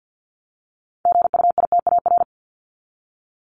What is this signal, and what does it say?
Morse code, heard as a keyed tone, about 48 wpm, 700 Hz. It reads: GUITAR
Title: Top 1501-1750 Words 50wpm